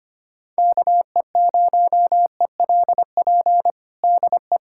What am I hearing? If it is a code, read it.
Morse code, heard as a keyed tone, about 25 wpm, 700 Hz. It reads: KE0ELP DE
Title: USA Callsign DE Callsign - 6 Characters - 25wpm